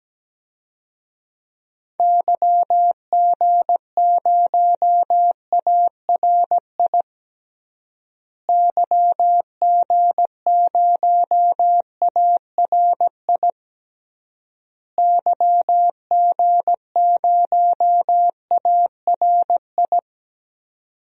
Morse code, heard as a keyed tone, about 17 wpm, 700 Hz. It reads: YG0ARI YG0ARI YG0ARI
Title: DXCC-100 Call Signs - 6 Characters - Repeated 3x - 17wpm